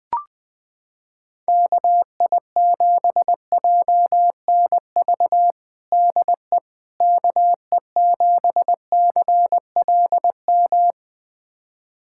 Morse code, heard as a keyed tone, about 20 wpm, 700 Hz. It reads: KI7JNV DE KE7CLM
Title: USA Callsign DE Callsign - 6 Characters - 20wpm